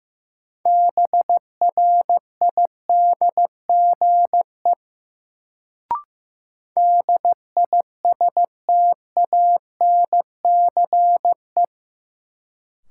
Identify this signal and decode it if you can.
Morse code, heard as a keyed tone, about 15 wpm, 700 Hz. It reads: BRIDGE DISTANCE
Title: Sentences from Top 1200 Words - New Word Drill Alt - 15wpm